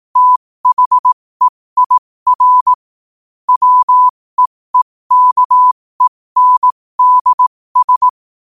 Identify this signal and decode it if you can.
Morse code, heard as a keyed tone, about 18 wpm, spaced as for 15 wpm, 1000 Hz. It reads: THEIR WEEKENDS
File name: qotdfiles / 15wpm